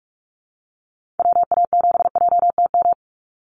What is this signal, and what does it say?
Morse code, heard as a keyed tone, about 44 wpm, 700 Hz. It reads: WA7JTM
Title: NA SOTA Activator Callsigns 45wpm